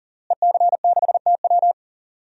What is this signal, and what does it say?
Morse code, heard as a keyed tone, about 37 wpm, 700 Hz. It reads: EC6TW
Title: DXCC-100 Call Signs - 5 Characters - Repeated 3x - 40wpm